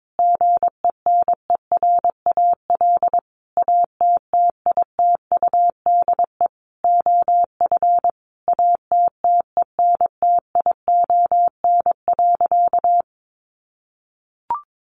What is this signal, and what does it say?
Morse code, heard as a keyed tone, about 22 wpm, 700 Hz. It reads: GENERAL ATTITUDE OF ATTENTION.
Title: Sentences from Top 1300 Words 22wpm